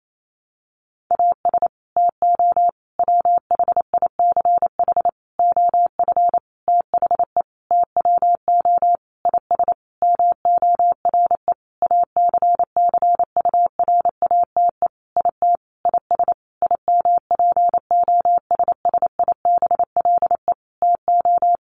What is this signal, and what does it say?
Morse code, heard as a keyed tone, about 28 wpm, 700 Hz. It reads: AS TO WHICH OF THE TWO IS MORE ACCURATE IT IS IMPOSSIBLE TO
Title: Sentences from Top 1200 Words 28wpm